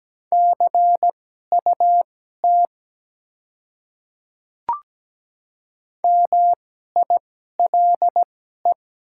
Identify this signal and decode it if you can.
Morse code, heard as a keyed tone, about 17 wpm, spaced as for 12 wpm, 690 Hz. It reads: CUT MILE
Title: 2x Character Spacing - Top 300 Words - 17wpm